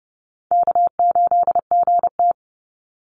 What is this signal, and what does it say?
Morse code, heard as a keyed tone, about 30 wpm, 700 Hz. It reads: K8GT
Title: Call Signs - 30wpm